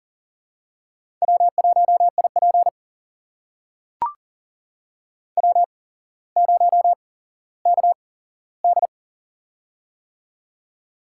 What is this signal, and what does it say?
Morse code, heard as a keyed tone, about 39 wpm, spaced as for 17 wpm, 700 Hz. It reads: W1IP W0KD
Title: USA Call Signs - 4 Characters - 8x Character Spacing - 40wpm